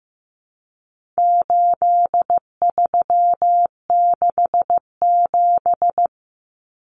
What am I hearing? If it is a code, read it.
Morse code, heard as a keyed tone, about 15 wpm, 700 Hz. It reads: 8367
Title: Common Cut Numbers - 4-Digit 15wpm